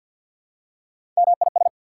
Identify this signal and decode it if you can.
Morse code, heard as a keyed tone, about 48 wpm, 690 Hz. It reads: MIS
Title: Prefixes and Suffixes 50wpm